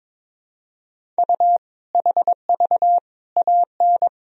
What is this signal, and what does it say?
Morse code, heard as a keyed tone, about 22 wpm, 700 Hz. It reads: U HV AN
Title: Sentences from Top 400 Words and Abbreviations 22wpm